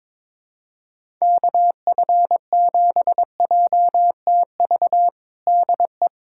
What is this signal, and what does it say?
Morse code, heard as a keyed tone, about 22 wpm, 700 Hz. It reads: KF7JTV DE
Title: USA Callsign DE Callsign 22wpm